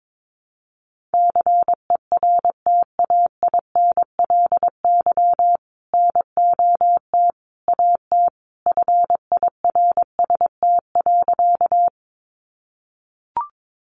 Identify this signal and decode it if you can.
Morse code, heard as a keyed tone, about 22 wpm, 700 Hz. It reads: CERTAINLY NOT AT FIRST.
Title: Sentences from Top 900 Words 22wpm